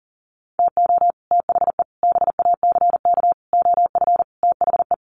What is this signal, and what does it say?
Morse code, heard as a keyed tone, about 40 wpm, 700 Hz. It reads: TO THE BACK OF THE
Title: Sets of 5 Words 40wpm